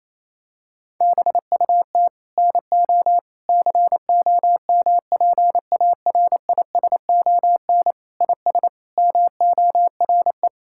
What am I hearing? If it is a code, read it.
Morse code, heard as a keyed tone, about 28 wpm, 700 Hz. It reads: BUT NO COMPARISON IS MORE
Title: Sentences from Top 1600 Words 28wpm